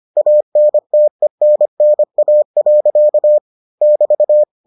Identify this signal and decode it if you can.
Morse code, heard as a keyed tone, about 25 wpm, 600 Hz. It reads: ANTENNA. <BT>